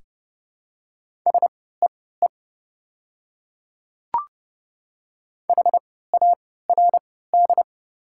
Morse code, heard as a keyed tone, about 30 wpm, spaced as for 17 wpm, 700 Hz. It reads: SEE HARD
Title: 3x Character Spacing - Top 300 Words - 30wpm